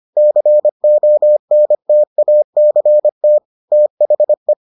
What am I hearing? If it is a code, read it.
Morse code, heard as a keyed tone, about 25 wpm, 600 Hz. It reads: CONTACT THE